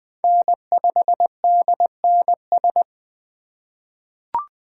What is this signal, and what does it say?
Morse code, heard as a keyed tone, about 20 wpm, 700 Hz. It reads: N5DNS
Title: USA Call Signs - 5 Characters - 2x Character Spacing - 20wpm